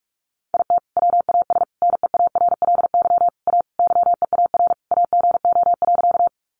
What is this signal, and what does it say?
Morse code, heard as a keyed tone, about 44 wpm, 700 Hz. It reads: IT WAS NEARLY A YEAR AGO.
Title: Sentences from Top 1300 Words 45wpm